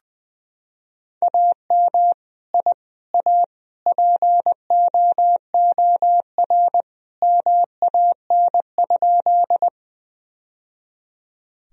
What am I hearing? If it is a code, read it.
Morse code, heard as a keyed tone, about 20 wpm, 700 Hz. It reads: AM I A POOR MAN?
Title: Sentences from Top 1000 Words - Review Part 2 - 20wpm